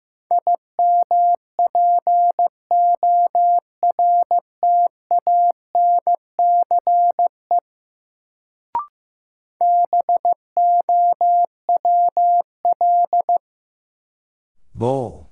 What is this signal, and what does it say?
Morse code, heard as a keyed tone, about 15 wpm, 700 Hz. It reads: IMPORTANCE BOWL
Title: Sentences from Top 1200 Words - New Word Drill Alt - 15wpm